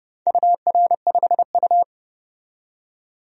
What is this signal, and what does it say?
Morse code, heard as a keyed tone, about 30 wpm, 700 Hz. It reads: UR5U
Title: DXCC-100 Call Signs - 4 Characters - 30wpm